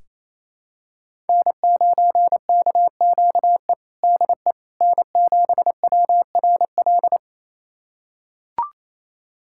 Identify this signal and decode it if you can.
Morse code, heard as a keyed tone, about 28 wpm, 700 Hz. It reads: N9KQE DE N7WRL
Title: USA Callsign DE Callsign - 5 Characters - 28wpm